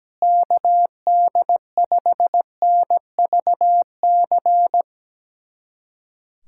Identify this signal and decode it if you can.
Morse code, heard as a keyed tone, about 17 wpm, 700 Hz. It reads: KD5NVC
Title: USA Callsign DE Callsign - 6 Characters - 17wpm